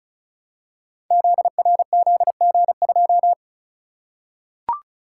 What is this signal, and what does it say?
Morse code, heard as a keyed tone, about 35 wpm, 700 Hz. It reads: ZRZG2
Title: Code Groups - 5 Letter:Numbers - 35wpm